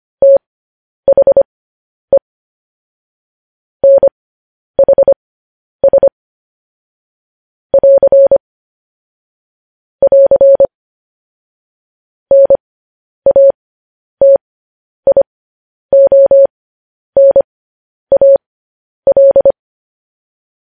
Morse code, heard as a keyed tone, about 25 wpm, spaced as for 10 wpm, 550 Hz. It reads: THE NHS <AR> <AR> NATIONAL
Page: news-headlines-in-morse-code-at-10-wpm